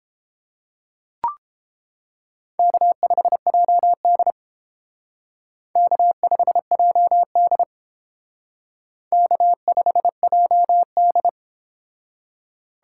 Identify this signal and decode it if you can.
Morse code, heard as a keyed tone, about 30 wpm, 700 Hz. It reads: K5JD K5JD K5JD